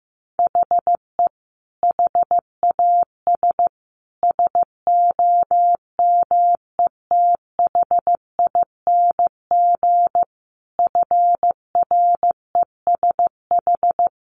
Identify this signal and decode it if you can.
Morse code, heard as a keyed tone, about 15 wpm, 700 Hz. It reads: HE HAS SOMETHING FRESH
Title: Sentences from Top 1200 Words 15wpm